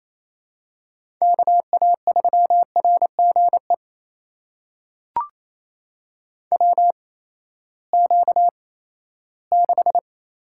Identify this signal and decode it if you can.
Morse code, heard as a keyed tone, about 28 wpm, spaced as for 16 wpm, 700 Hz. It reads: KA3RGE WQ6
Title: USA Call Signs - 8x Character Spacing - 28wpm